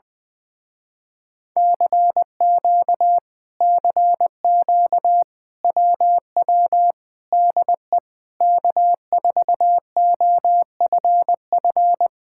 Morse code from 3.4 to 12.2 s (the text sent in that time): CQ WW DE K4OFF